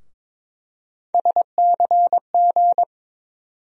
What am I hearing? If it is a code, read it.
Morse code, heard as a keyed tone, about 21 wpm, 700 Hz. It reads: SCG
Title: USA Call Sign Postfixes - 3 Letters 22wpm